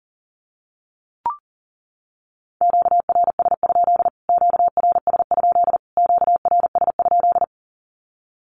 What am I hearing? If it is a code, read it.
Morse code, heard as a keyed tone, about 40 wpm, 700 Hz. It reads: QRS? QRS? QRS?
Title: Q-Codes Repeated 3x 40wpm